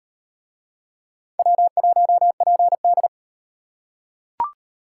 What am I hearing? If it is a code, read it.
Morse code, heard as a keyed tone, about 38 wpm, 700 Hz. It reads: W1PD